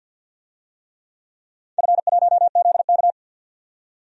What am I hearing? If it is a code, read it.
Morse code, heard as a keyed tone, about 49 wpm, 700 Hz. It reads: F1ZK
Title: DXCC-100 Call Signs - 4 Characters - 50wpm